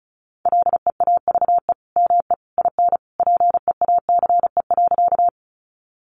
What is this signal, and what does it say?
Morse code, heard as a keyed tone, about 34 wpm, 700 Hz. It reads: LEAVE ME IN PEACE.
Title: Sentences from Top 1500 Words 35wpm